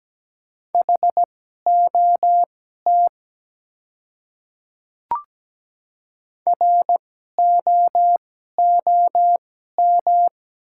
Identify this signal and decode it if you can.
Morse code, heard as a keyed tone, about 17 wpm, spaced as for 12 wpm, 700 Hz. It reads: HOT ROOM
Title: 2x Character Spacing - Top 300 Words - 17wpm